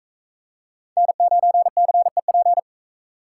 Morse code, heard as a keyed tone, about 44 wpm, 700 Hz. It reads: N9CEP